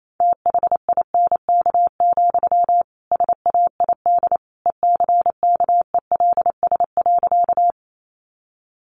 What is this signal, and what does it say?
Morse code, heard as a keyed tone, about 28 wpm, 700 Hz. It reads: THINK, SAID ECKELS.
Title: A Sound Of Thunder 28wpm